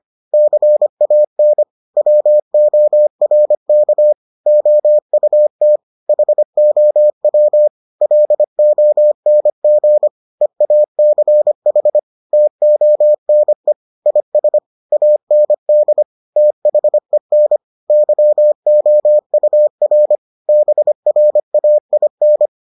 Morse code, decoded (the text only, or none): CAN WORK OUT HOW LONG EACH TONE IS AND THEN YOUR BRAIN